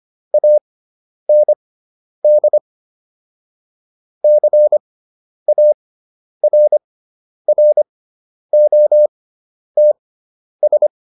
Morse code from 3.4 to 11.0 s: CARROTS

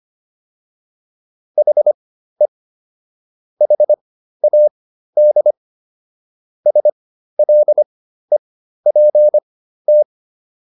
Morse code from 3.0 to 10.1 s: HAD SLEPT

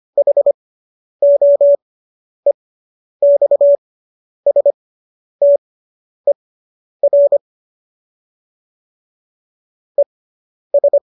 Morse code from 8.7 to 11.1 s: ES

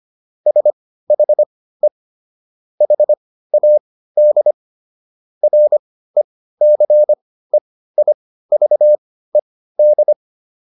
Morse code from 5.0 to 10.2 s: RECEIVED

